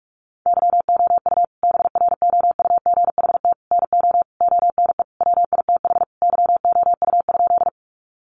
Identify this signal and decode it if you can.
Morse code, heard as a keyed tone, about 44 wpm, 700 Hz. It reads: YOU BROUGHT NO ONE WITH YOU?
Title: Sentences from Top 500 Words 45wpm